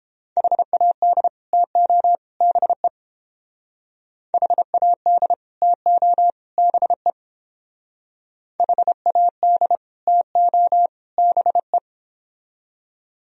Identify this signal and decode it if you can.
Morse code, heard as a keyed tone, about 31 wpm, 700 Hz. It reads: HAD TO BE HAD TO BE HAD TO BE